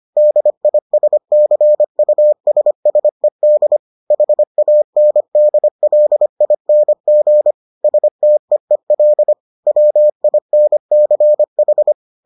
Morse code, heard as a keyed tone, about 25 wpm, 600 Hz. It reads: DISCUSSED HANDLING STEEL WINCH